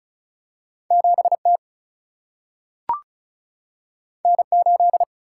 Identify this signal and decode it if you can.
Morse code, heard as a keyed tone, about 36 wpm, 700 Hz. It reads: 7T N8
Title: Single-Letter Single-Number 35wpm